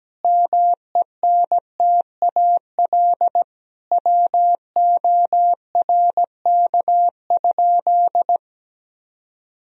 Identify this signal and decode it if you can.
Morse code, heard as a keyed tone, about 17 wpm, 700 Hz. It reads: MENTAL WORK?
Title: Sentences from Top 1000 Words - Review Part 4 - 17wpm